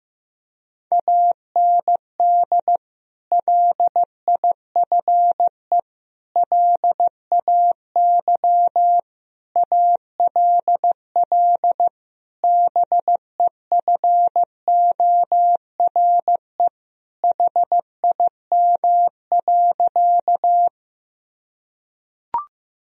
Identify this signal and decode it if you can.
Morse code, heard as a keyed tone, about 15 wpm, 700 Hz. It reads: AND LIFE LAY ALL BEFORE HIM.
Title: Sentences from Top 1000 Words - Review Part 2 - 15wpm